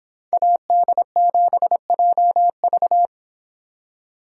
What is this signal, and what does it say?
Morse code, heard as a keyed tone, about 26 wpm, 700 Hz. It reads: AD7JV